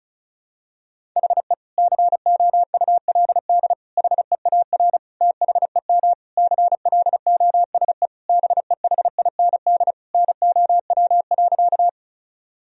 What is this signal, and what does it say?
Morse code, heard as a keyed tone, about 34 wpm, 700 Hz. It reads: HE COULD HEAR THEM CLOSE BEHIND NOW.
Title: Sentences from Top 1000 Words - Review Part 1 - 35wpm